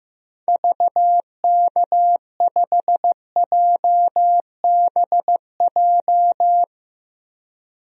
Morse code, heard as a keyed tone, about 15 wpm, 700 Hz. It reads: VK5JBJ
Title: DXCC-100 Call Signs - 6 Characters - 15wpm